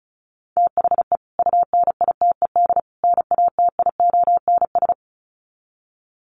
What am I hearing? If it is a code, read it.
Morse code, heard as a keyed tone, about 34 wpm, 700 Hz. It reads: THE UNITED NATIONS
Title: Sets of 3 Words 35wpm